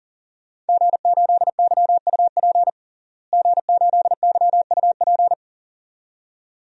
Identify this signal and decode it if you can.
Morse code, heard as a keyed tone, about 37 wpm, 700 Hz. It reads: G8YUP G8YUP
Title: DXCC-100 Call Signs - 5 Characters - Repeated 2x - 40wpm